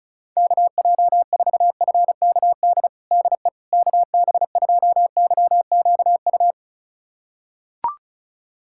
Morse code, heard as a keyed tone, about 34 wpm, 700 Hz. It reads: KJ4FKD DE KB2YQU
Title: USA Callsign DE Callsign 35wpm